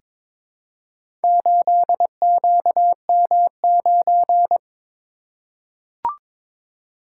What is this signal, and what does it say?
Morse code, heard as a keyed tone, about 21 wpm, 700 Hz. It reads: 8QM9
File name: practice / 22wpm / Four Letter-Number Combinations - 22wpm